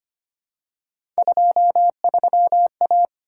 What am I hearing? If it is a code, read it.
Morse code, heard as a keyed tone, about 25 wpm, 700 Hz. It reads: 23A
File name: Three Letter-Number Combinations - 25wpm